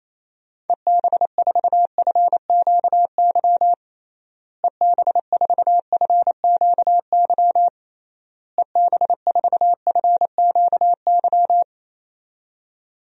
Morse code, heard as a keyed tone, about 27 wpm, 700 Hz. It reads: EB4FQY EB4FQY EB4FQY
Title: DXCC-100 Call Signs - 6 Characters - Repeated 3x - 28wpm